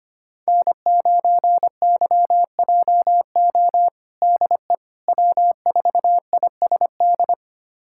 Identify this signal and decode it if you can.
Morse code, heard as a keyed tone, about 25 wpm, 700 Hz. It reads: N9YJO DE W4ISD
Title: USA Callsign DE Callsign - 5 Characters - 25wpm